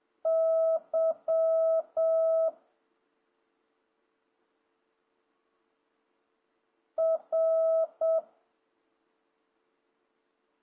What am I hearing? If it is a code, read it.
Morse code, heard as a keyed tone, about 7 wpm, 650 Hz. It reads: Y R
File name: jh8cbh / 2026cw